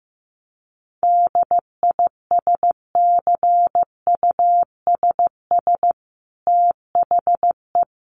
Morse code, heard as a keyed tone, about 15 wpm, 700 Hz. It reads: DISCUSS THE
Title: Sentences from Top 1500 Words 15wpm